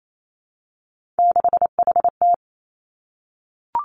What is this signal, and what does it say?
Morse code, heard as a keyed tone, about 28 wpm, 700 Hz. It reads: 6HT
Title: Three Challenging Letter-Number Combinations 28wpm